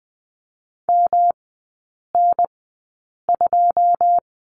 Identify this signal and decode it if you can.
Morse code, heard as a keyed tone, about 20 wpm, spaced as for 8 wpm, 700 Hz. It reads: MN2
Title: 107 Lesson 18e - Letter m - Three Characters 20wpm